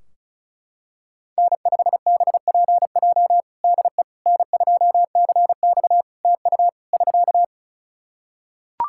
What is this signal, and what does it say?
Morse code, heard as a keyed tone, about 34 wpm, 700 Hz. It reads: N5BPJ DE N2CX TU <SK>
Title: K1USN SST QSO Practice 35wpm